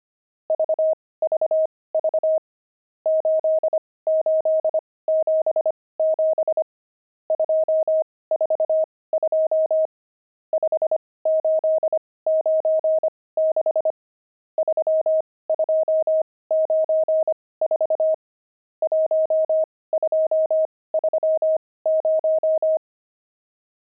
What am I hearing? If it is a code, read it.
Morse code, heard as a keyed tone, about 25 wpm, spaced as for 18 wpm, 630 Hz. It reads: VVV 8877 242 5896 3294 1230
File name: pangram-fs-18-cs-25-nums19